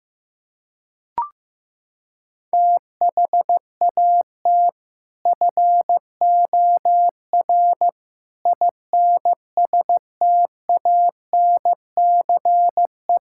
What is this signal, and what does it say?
Morse code, heard as a keyed tone, about 15 wpm, 700 Hz. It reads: THAT FOR INSTANCE
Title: Sentences from Top 1000 Words - Review Part 5 - 15wpm